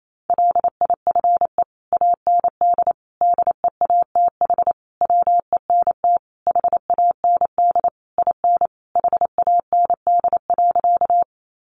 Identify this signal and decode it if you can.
Morse code, heard as a keyed tone, about 28 wpm, 700 Hz. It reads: LIFE AND DEATH WENT HAND IN HAND.
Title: Sentences from Top 1100 Words 28wpm